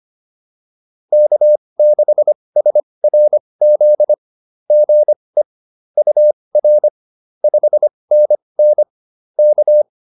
Morse code, heard as a keyed tone, about 25 wpm, spaced as for 20 wpm, 600 Hz. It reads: K6SRZ GE UR 5NN K